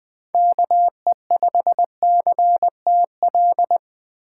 Morse code, heard as a keyed tone, about 20 wpm, 700 Hz. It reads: KE5CTL
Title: USA Call Signs - 6 Characters - 2x Character Spacing - 20wpm